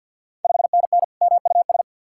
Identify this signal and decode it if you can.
Morse code, heard as a keyed tone, about 52 wpm, 700 Hz. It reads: 5NN MUS
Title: Ontario QSO Party Exchange 50wpm